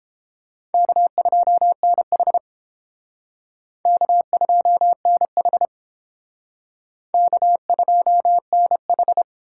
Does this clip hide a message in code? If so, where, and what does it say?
Yes: Morse code from 0.7 to 9.2 s, heard as a keyed tone, about 30 wpm, 700 Hz. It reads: K2NH K2NH K2NH